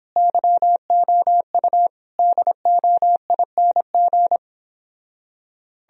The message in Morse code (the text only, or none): YOU DOING